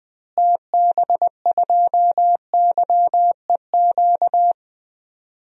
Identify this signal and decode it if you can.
Morse code, heard as a keyed tone, about 20 wpm, 700 Hz. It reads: TB2YEQ
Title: DXCC-100 Call Signs - 6 Characters - Repeated 2x - 20wpm